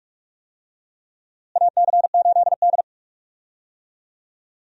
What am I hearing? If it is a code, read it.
Morse code, heard as a keyed tone, about 46 wpm, 700 Hz. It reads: AC8D
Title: USA Call Signs - 4 Characters - Repeated 3x - 45wpm